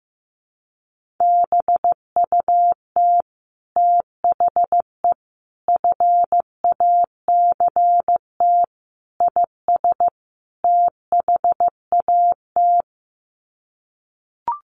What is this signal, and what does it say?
Morse code, heard as a keyed tone, about 15 wpm, 700 Hz. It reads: BUT THE FACT IS THAT
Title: Sets of 5 Words Encore2 15wpm